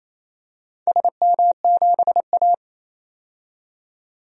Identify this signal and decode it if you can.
Morse code, heard as a keyed tone, about 28 wpm, 700 Hz. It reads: SM7A
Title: DXCC-100 Call Signs - 4 Characters - 28wpm